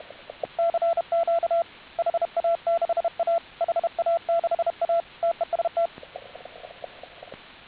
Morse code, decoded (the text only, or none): CQ HA6A HA6A TEST